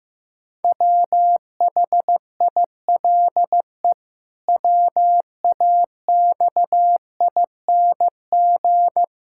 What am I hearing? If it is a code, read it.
Morse code, heard as a keyed tone, about 15 wpm, 700 Hz. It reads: WHILE WAXING
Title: Pangram Challenge v2 - 15wpm